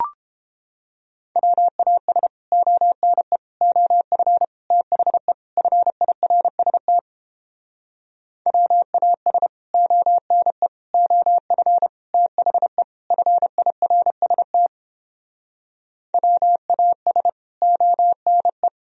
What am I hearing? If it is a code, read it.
Morse code, heard as a keyed tone, about 31 wpm, 700 Hz. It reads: WAS ONE OF THE FIRST WAS ONE OF THE FIRST WAS ONE